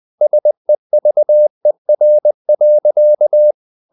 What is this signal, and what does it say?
Morse code, heard as a keyed tone, about 20 wpm, 600 Hz. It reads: SEVER.
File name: the old man and the sea part 2 20wpm